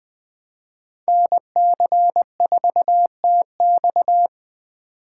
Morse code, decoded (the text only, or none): NC4TX